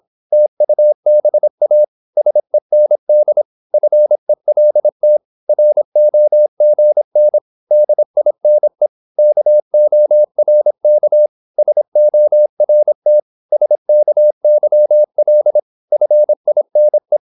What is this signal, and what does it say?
Morse code, heard as a keyed tone, about 26 wpm, 600 Hz. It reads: TUBA SEND FELT ROGN DINE KORK SORT SKYL FINE
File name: Korte ord DK 26wpm